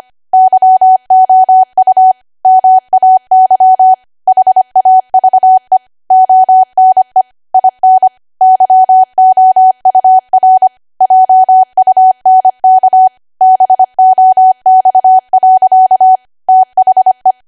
Morse code, heard as a keyed tone, about 25 wpm, 750 Hz. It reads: YOU MAY HAVE ONE IN YOUR JUNK BOX. THE